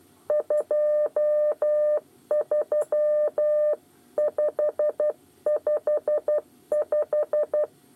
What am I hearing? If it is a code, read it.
Morse code, heard as a keyed tone, about 12 wpm, 600 Hz. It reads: 23 555
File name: 8 bis lezione cw 2